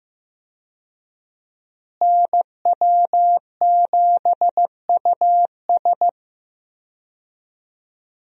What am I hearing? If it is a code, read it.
Morse code, heard as a keyed tone, about 15 wpm, 700 Hz. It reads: NW7US
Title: USA Call Signs - 5 Characters - Repeated 3x - 15wpm